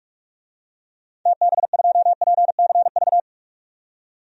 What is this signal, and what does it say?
Morse code, heard as a keyed tone, about 46 wpm, 690 Hz. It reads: TB2PCV
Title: DXCC-100 Call Signs - 6 Characters - 45wpm